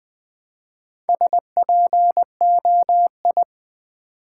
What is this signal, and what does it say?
Morse code, heard as a keyed tone, about 20 wpm, 700 Hz. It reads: SPOI
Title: Build A Word - 1 Letter At A Time - 5-Letter Words - 20wpm